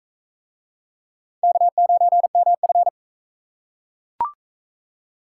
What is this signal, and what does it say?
Morse code, heard as a keyed tone, about 43 wpm, 700 Hz. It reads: K9MF